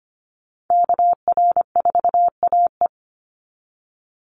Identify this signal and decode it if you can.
Morse code, heard as a keyed tone, about 25 wpm, 700 Hz. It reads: KR4AE
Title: Call Signs Repeated 2x - 25wpm